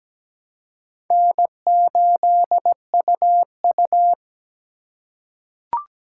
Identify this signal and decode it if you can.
Morse code, heard as a keyed tone, about 17 wpm, 700 Hz. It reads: N8UU
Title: USA Call Signs - 4 Characters - 3x Character Spacing - 17wpm